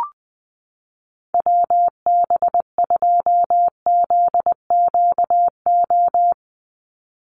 Morse code, heard as a keyed tone, about 20 wpm, 700 Hz. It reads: WB2ZQO
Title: USA Call Signs 20wpm